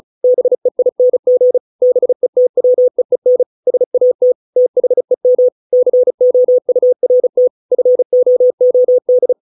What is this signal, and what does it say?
Morse code, heard as a keyed tone, about 34 wpm, 500 Hz. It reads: BEING BETWEEN SAT THEM COURT FOOD